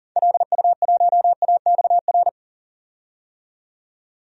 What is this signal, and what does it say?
Morse code, heard as a keyed tone, about 40 wpm, 700 Hz. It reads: LU1AXR
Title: DXCC-100 Call Signs - 6 Characters - 40wpm